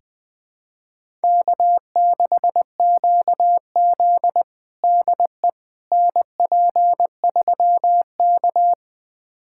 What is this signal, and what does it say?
Morse code, heard as a keyed tone, about 20 wpm, 700 Hz. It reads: K6QZ DE NP3K
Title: USA Callsign DE Callsign - 4 Characters - 20wpm